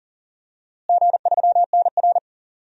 Morse code, heard as a keyed tone, about 40 wpm, 700 Hz. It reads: G3NR
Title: DXCC-100 Call Signs - 4 Characters - Repeated 3x - 40wpm